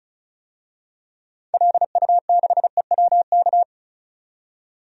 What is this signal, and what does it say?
Morse code, heard as a keyed tone, about 36 wpm, 700 Hz. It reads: LU6EWK
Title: DXCC-100 Call Signs - 6 Characters - 35wpm